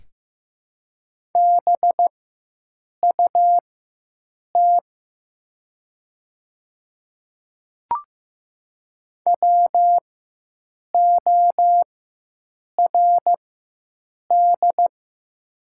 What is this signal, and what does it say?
Morse code, heard as a keyed tone, about 15 wpm, spaced as for 7 wpm, 700 Hz. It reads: BUT WORD
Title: 4x Character Spacing - Top 100 Words - 15wpm